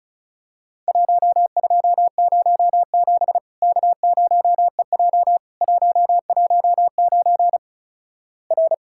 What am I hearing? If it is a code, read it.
Morse code, heard as a keyed tone, about 36 wpm, 700 Hz. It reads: 1207 K0EJ 119 R
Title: Worked All Europe DX Contest - QTC Exchange - 35wpm